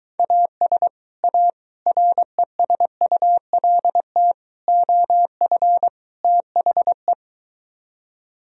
Morse code, heard as a keyed tone, about 22 wpm, 700 Hz. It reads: AS A RESULT OF THE